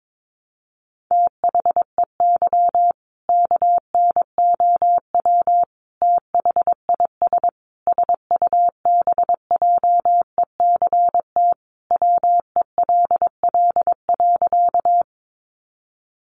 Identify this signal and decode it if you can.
Morse code, heard as a keyed tone, about 22 wpm, 700 Hz. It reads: THEY KNOW THIS SUBJECT WELL.
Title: Sentences from Top 800 Words 22wpm